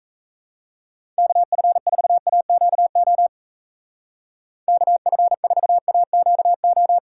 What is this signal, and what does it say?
Morse code, heard as a keyed tone, about 34 wpm, 700 Hz. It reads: KF4AQO KF4AQO
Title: Speed Racing - USA 2x3 Call Signs 28wpm